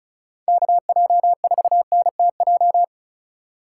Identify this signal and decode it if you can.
Morse code, heard as a keyed tone, about 36 wpm, 700 Hz. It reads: KJ4NTJ
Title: USA Call Signs - 6 Characters - 4x Character Spacing - 35wpm